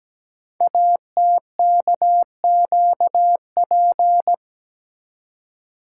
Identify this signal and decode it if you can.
Morse code, heard as a keyed tone, about 17 wpm, 700 Hz. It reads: ATKQP
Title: Code Groups - 5 Letters - 17wpm